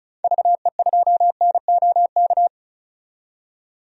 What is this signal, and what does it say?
Morse code, heard as a keyed tone, about 34 wpm, 700 Hz. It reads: VE2NOK
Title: DXCC-100 Call Signs - 6 Characters - 35wpm